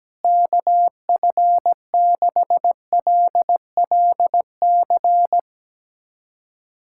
Morse code, heard as a keyed tone, about 17 wpm, 700 Hz. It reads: KF6LLC
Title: USA Callsign DE Callsign - 6 Characters - 17wpm